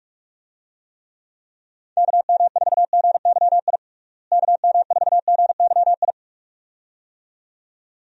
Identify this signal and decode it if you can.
Morse code, heard as a keyed tone, about 46 wpm, 700 Hz. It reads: KM4GYI KM4GYI
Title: USA Call Signs - 6 Characters - Repeated 2x - 45wpm